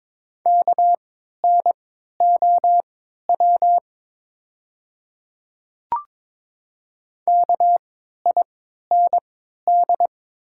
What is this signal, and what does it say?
Morse code, heard as a keyed tone, about 22 wpm, spaced as for 12 wpm, 690 Hz. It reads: KNOW KIND
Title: 3x Character Spacing - Top 200 Words - 22wpm